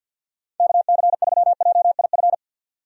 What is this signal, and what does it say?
Morse code, heard as a keyed tone, about 53 wpm, 700 Hz. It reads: KC3JIF
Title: USA Call Signs - 6 Characters - 50wpm